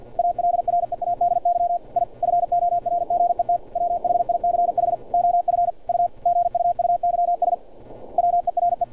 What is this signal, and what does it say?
Morse code, heard as a keyed tone, about 50 wpm, 700 Hz. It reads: ALREADY I FORGET WHICH OF U GUUYS WERE
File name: archived / CRSnet